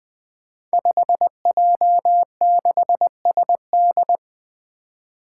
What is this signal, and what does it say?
Morse code, heard as a keyed tone, about 21 wpm, 700 Hz. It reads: 5J6SD
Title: Five Challenging Letter-Number Combinations 20wpm